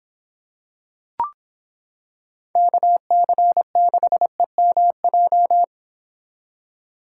Morse code, heard as a keyed tone, about 26 wpm, 700 Hz. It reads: KC6EMJ